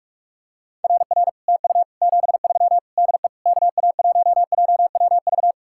Morse code, heard as a keyed tone, about 36 wpm, 700 Hz. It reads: RR TU 73 DE KA1JWV
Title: Speed-Racing Common QSO Phrases - 30wpm